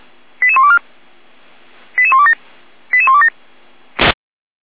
Five-tone selective call (ZVEI), heard as quick runs of five tones, tones of about 70 ms. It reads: 88124 88127 88127
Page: SDR/ZVEI